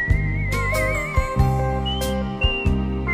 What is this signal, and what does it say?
There is no code.